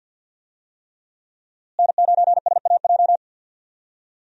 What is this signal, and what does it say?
Morse code, heard as a keyed tone, about 52 wpm, 700 Hz. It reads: N9SAJ
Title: USA Call Signs - 5 Characters - Repeated 3x - 50wpm